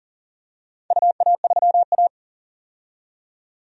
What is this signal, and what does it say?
Morse code, heard as a keyed tone, about 40 wpm, 700 Hz. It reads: UA3A
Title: DXCC-100 Call Signs - 4 Characters - 40wpm